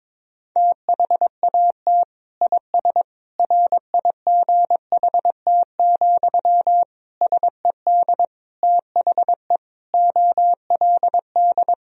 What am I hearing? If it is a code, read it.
Morse code, heard as a keyed tone, about 22 wpm, 700 Hz. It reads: THAT IS RIGHT, SED THE OLD
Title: Sentences from Top 200 Words and Abbreviations 22wpm